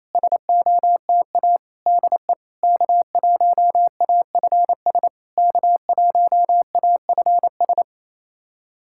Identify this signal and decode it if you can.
Morse code, heard as a keyed tone, about 28 wpm, 690 Hz. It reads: SOTA DE K1AFS K1AFS
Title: SOTA QSO Practice 28wpm